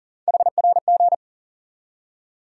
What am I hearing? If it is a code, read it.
Morse code, heard as a keyed tone, about 39 wpm, 700 Hz. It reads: HRG
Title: USA Call Sign Postfixes - 3 Letters 40wpm